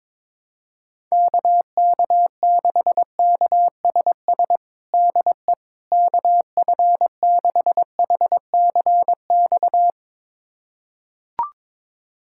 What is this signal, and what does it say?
Morse code, heard as a keyed tone, about 22 wpm, 700 Hz. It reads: KK6KSS DE KF6HCX
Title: USA Callsign DE Callsign - 6 Characters - 22wpm